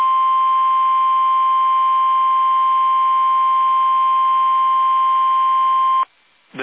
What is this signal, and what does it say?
Emergency Alert System (SAME) tones (the 1050 Hz attention tone). It is heard from 0.4 to 5.8 s.
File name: eas / KUSO / KUSO